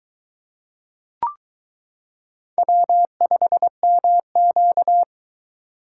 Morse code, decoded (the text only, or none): W5MQ